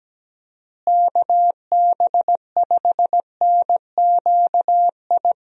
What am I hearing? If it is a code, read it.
Morse code, heard as a keyed tone, about 17 wpm, 700 Hz. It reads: KB5NQI
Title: USA Call Signs - 6 Characters - Repeated 2x - 17wpm